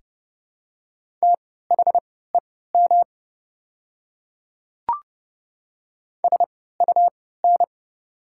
Morse code, heard as a keyed tone, about 30 wpm, spaced as for 17 wpm, 700 Hz. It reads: THEM SUN
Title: 3x Character Spacing - Top 500 Words - 30wpm